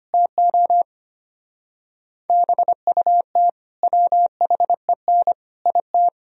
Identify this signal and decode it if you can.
Morse code, heard as a keyed tone, about 32 wpm, 700 Hz. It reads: TO BUT WHEN IT